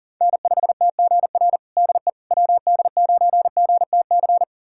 Morse code, heard as a keyed tone, about 37 wpm, 700 Hz. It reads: N5TGR DE WD9GTC
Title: USA Callsign DE Callsign 40wpm